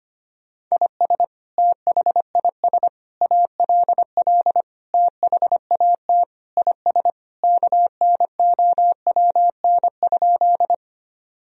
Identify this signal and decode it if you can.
Morse code, heard as a keyed tone, about 25 wpm, 700 Hz. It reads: IS THIS ALL THAT IS KNOWN?
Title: Sentences from Top 1000 Words - Review Part 3 - 25wpm